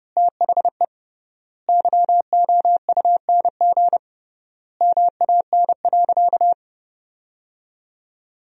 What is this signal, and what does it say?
Morse code, heard as a keyed tone, about 32 wpm, 700 Hz. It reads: THE YOUNG MAN.
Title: 3x Word-Spacing - Sentences from Top 400 Words - 30wpm